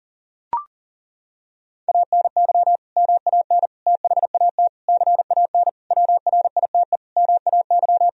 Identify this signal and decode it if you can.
Morse code, heard as a keyed tone, about 40 wpm, 700 Hz. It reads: ANY MAN THAT CAN WRITE MAY
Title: Sentences from Top 1000 Words - Review Part 1 - 40wpm